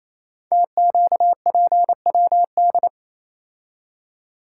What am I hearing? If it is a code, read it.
Morse code, heard as a keyed tone, about 28 wpm, 700 Hz. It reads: TQPWD
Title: Code Groups - 5 Letters - 28wpm